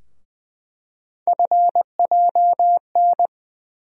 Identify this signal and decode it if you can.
Morse code, heard as a keyed tone, about 20 wpm, 700 Hz. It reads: FJN
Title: USA Call Sign Postfixes - 3 Letters 20wpm